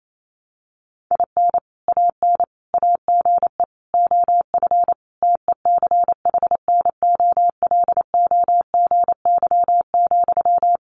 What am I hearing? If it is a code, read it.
Morse code, heard as a keyed tone, about 28 wpm, 700 Hz. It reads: IN AN AGE OF TECHNOLOGY,
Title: Sentences from Top 1200 Words 28wpm